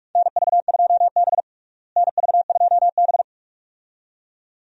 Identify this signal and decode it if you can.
Morse code, heard as a keyed tone, about 46 wpm, 700 Hz. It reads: NV2B NV2B
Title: USA Call Signs - 4 Characters - Repeated 2x - 45wpm